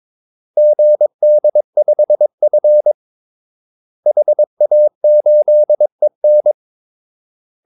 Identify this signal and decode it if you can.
Morse code, heard as a keyed tone, about 22 wpm, 600 Hz. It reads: GD5F HA8EN
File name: Callsigns-22wpm